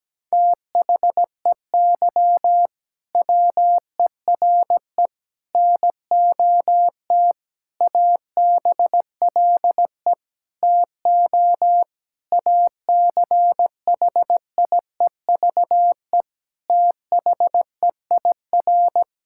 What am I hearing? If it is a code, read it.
Morse code, heard as a keyed tone, about 17 wpm, 700 Hz. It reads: THEY WERE NOT ABLE TO ACHIEVE THEIR
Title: Sentences from Top 1000 Words - Review Part 6 - 17wpm